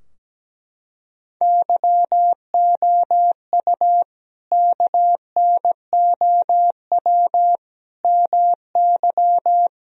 Morse code, heard as a keyed tone, about 17 wpm, 700 Hz. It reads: YOU KNOW MY
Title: Sentences from Top 1000 Words - Review Part 3 - 17wpm